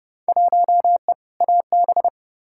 Morse code, heard as a keyed tone, about 31 wpm, 700 Hz. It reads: 1E AB
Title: Field Day Exchanges - 30wpm